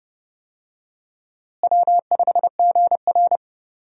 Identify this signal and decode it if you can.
Morse code, heard as a keyed tone, about 30 wpm, 700 Hz. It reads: W5GR
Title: USA Call Signs - 4 Characters - Repeated 3x - 30wpm